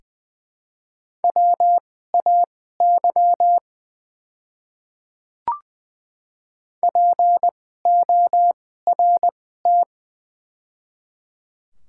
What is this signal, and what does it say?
Morse code, heard as a keyed tone, about 21 wpm, spaced as for 15 wpm, 700 Hz. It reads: WAY PORT
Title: 2x Character Spacing - Top 200 Words - 20wpm